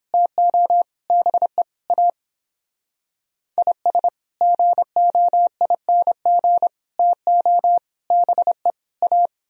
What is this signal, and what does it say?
Morse code, heard as a keyed tone, about 26 wpm, 700 Hz. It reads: TO BE A IS GOING TO BE A
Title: Speed-Racing - Sets of 5 Words 22wpm